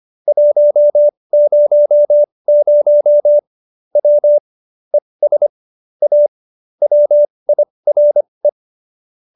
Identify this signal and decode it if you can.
Morse code, heard as a keyed tone, about 25 wpm, spaced as for 20 wpm, 590 Hz. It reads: 100 W ES A WIRE